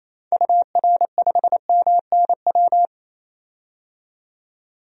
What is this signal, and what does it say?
Morse code, heard as a keyed tone, about 27 wpm, 700 Hz. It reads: UR5MNW
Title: DXCC-100 Call Signs - 6 Characters - 28wpm